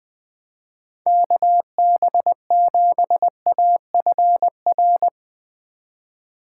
Morse code, heard as a keyed tone, about 20 wpm, 700 Hz. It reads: KB7AFR
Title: Speed Racing - USA 2x3 Call Signs 20wpm